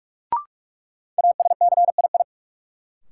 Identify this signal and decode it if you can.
Morse code, heard as a keyed tone, about 46 wpm, 700 Hz. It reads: ASCII
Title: Sets of 5 Words - New Word Drill - 45wpm